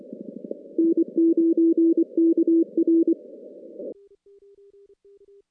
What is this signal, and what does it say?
Morse code, heard as a keyed tone, about 24 wpm, 330 Hz. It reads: N9KR
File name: contests / CW-SS-k5tr- -2100